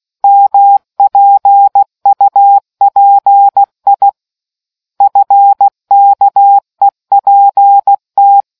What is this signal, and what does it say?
Morse code, heard as a keyed tone, about 16 wpm, 800 Hz. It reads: MPUPI FKEPT